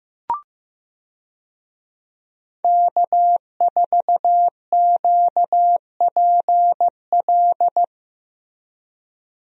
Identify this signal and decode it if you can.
Morse code, heard as a keyed tone, about 15 wpm, 700 Hz. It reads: K4QPL
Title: USA Call Signs - 5 Characters - Repeated 2x - 15wpm